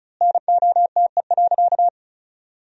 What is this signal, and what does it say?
Morse code, heard as a keyed tone, about 36 wpm, 700 Hz. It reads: NOTE.
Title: 3x Word-Spacing - Sentences from Top 500 Words - 35wpm